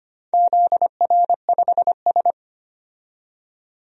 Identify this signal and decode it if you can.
Morse code, heard as a keyed tone, about 25 wpm, 700 Hz. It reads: ZR5S
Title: DXCC-100 Call Signs - 4 Characters - 25wpm